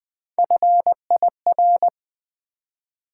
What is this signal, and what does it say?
Morse code, heard as a keyed tone, about 20 wpm, 700 Hz. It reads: FIR
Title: Build A Word - 3 Letters At A Time - 8-Letter Words - 20wpm